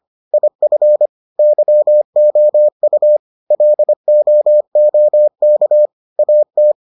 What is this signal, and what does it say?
Morse code, heard as a keyed tone, about 25 wpm, 600 Hz. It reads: IF YOU LOOK AT